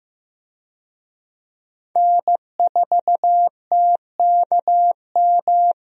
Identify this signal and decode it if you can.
Morse code, heard as a keyed tone, about 15 wpm, 700 Hz. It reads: N4TKM
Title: USA Call Signs - 5 Characters - Repeated 3x - 15wpm